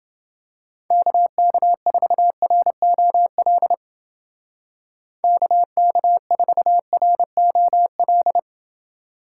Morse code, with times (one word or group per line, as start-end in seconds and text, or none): 0.9-3.7 KK4ROL
5.2-8.4 KK4ROL